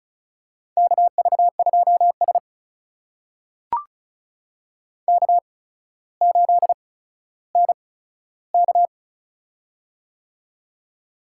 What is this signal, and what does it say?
Morse code, heard as a keyed tone, about 36 wpm, spaced as for 15 wpm, 700 Hz. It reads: KV2S K8NK
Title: USA Call Signs - 4 Characters - 8x Character Spacing - 35wpm